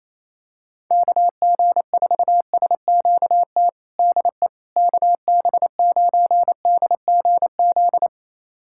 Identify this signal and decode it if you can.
Morse code, heard as a keyed tone, about 28 wpm, 700 Hz. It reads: KG4SQT DE KB9DGZ
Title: USA Callsign DE Callsign - 6 Characters - 28wpm